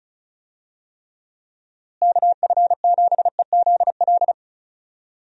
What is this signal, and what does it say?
Morse code, heard as a keyed tone, about 36 wpm, 700 Hz. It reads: KF7EZL
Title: USA Call Signs - 6 Characters - Repeated 2x - 35wpm